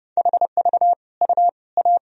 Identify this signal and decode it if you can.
Morse code, heard as a keyed tone, about 30 wpm, 700 Hz. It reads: HV U A
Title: Sentences from Top 400 Words and Abbreviations 30wpm